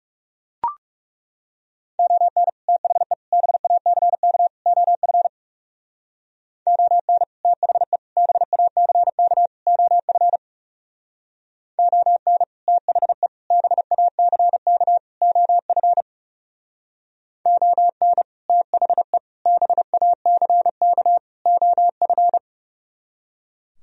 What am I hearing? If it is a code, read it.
Morse code, heard as a keyed tone, about 38 wpm, 700 Hz. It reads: ON THE BACK OF ON THE BACK OF ON THE BACK OF ON THE BACK OF